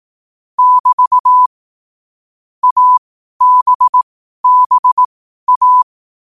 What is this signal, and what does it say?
Morse code, heard as a keyed tone, about 18 wpm, spaced as for 12 wpm, 1000 Hz. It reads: <BT> ABBA
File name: qotdfiles / 12wpm